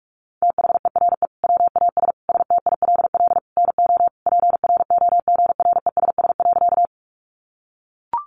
Morse code, heard as a keyed tone, about 44 wpm, 700 Hz. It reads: THERE WAS STILL NO PROGRESS.
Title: Sentences from Top 1400 Words 45wpm